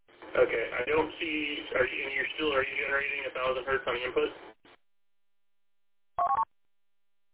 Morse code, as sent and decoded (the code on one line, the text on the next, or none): none